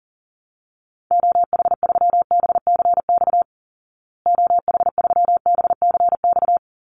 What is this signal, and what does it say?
Morse code, heard as a keyed tone, about 40 wpm, 700 Hz. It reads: OH3BCX OH3BCX
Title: Call Signs Repeated 2x - 40wpm